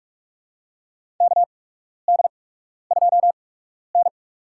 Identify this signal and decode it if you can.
Morse code, heard as a keyed tone, about 46 wpm, spaced as for 12 wpm, 700 Hz. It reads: KD2N
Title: USA Call Signs - 4 Characters - 8x Character Spacing - 45wpm